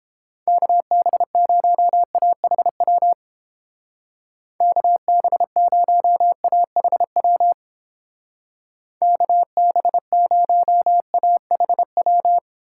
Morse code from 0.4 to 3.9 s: KB0AHW

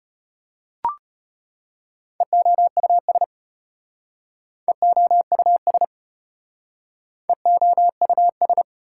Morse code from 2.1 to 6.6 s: EOUS EOUS

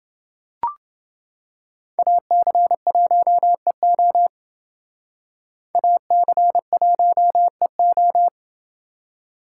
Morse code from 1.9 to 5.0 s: AC1EO

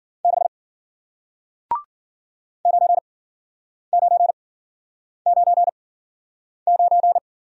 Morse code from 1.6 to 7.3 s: 9999

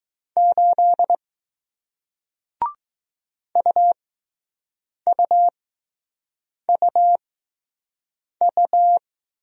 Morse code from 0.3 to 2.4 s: 8

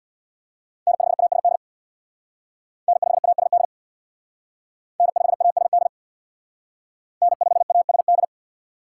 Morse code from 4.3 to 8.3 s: N5ASD N5ASD

